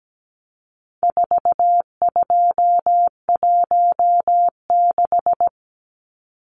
Morse code, heard as a keyed tone, about 17 wpm, 700 Hz. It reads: 4216
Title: Numbers - 4-Digit 17wpm